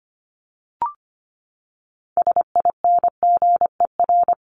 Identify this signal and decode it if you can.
Morse code, heard as a keyed tone, about 25 wpm, 700 Hz. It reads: SINGER
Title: Top 1501-1750 Words 25wpm